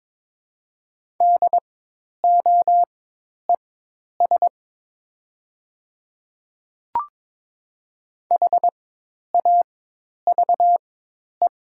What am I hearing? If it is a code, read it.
Morse code, heard as a keyed tone, about 21 wpm, spaced as for 10 wpm, 700 Hz. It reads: DOES HAVE
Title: 4x Character Spacing - Top 300 Words - 22wpm